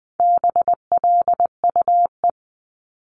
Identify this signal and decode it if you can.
Morse code, heard as a keyed tone, about 20 wpm, 700 Hz. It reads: BLUE
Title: Multi-letter Phonemes 20wpm